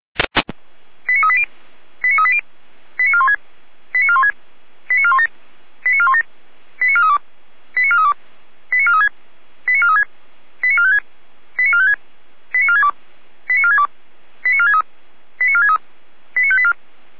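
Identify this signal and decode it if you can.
Five-tone selective call (ZVEI), heard as quick runs of five tones, tones of about 70 ms. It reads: 89380 89380 89416 89416 89417 89417 89432 89432 89436 89436 89456 89456 89462 89462 89463 89463 89584
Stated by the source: ZVEI five-tone call received by SDR.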